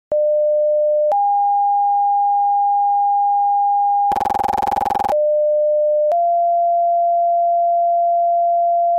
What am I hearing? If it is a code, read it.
Two-tone page, heard as a one-second tone, then a three-second tone, about 600 then 810 Hz, about 600 then 690 Hz. It is heard from 0.1 to 9.0 s.